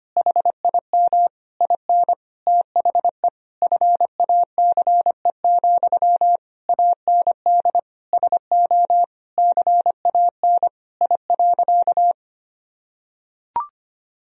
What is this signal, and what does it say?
Morse code, heard as a keyed tone, about 25 wpm, 700 Hz. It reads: HIM IN THE FACE, AND SO CAN I.
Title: Sentences from Top 1000 Words - Review Part 2 - 25wpm